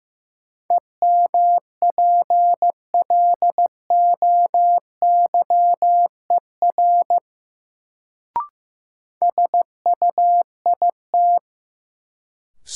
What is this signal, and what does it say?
Morse code, heard as a keyed tone, about 15 wpm, 700 Hz. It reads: EMPLOYER SUIT
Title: Sentences from Top 1600 Words - New Word Drill - 15wpm